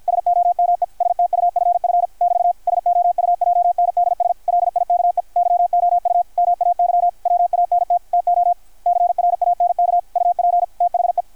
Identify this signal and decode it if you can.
Morse code, heard as a keyed tone, about 53 wpm, 720 Hz. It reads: SOME STUFF <BT> SOUJNDS LIKE YOU MAY WANT TO CRANK UP THE
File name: archived / CRSnet